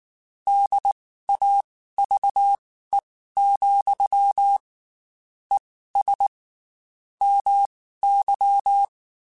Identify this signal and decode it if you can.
Morse code, heard as a keyed tone, about 19 wpm, spaced as for 14 wpm, 800 Hz. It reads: DAVE, ES MY
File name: QSOs / QSOs-mp3s / WD0BGP 1@18